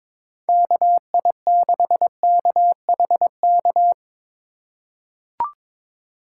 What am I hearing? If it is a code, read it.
Morse code, heard as a keyed tone, about 22 wpm, 700 Hz. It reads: KI6KHK
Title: USA Call Signs - 6 Characters - 2x Character Spacing - 22wpm